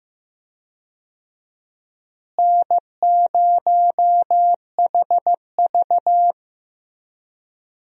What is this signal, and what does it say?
Morse code, heard as a keyed tone, about 15 wpm, 700 Hz. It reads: N0HV
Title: USA Call Signs - 4 Characters - Repeated 3x - 15wpm